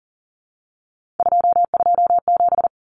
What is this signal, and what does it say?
Morse code, heard as a keyed tone, about 40 wpm, 700 Hz. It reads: 227
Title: Numbers - 3-Digit 40wpm